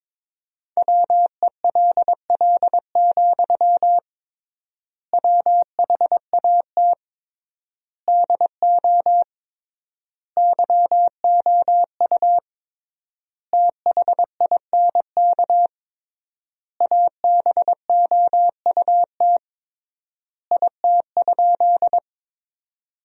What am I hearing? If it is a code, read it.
Morse code, heard as a keyed tone, about 22 wpm, 700 Hz. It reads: WELL, WHAT DO YOU THINK ABOUT IT?
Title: 3x Word-Spacing - Sentences from Top 200 Words - 22wpm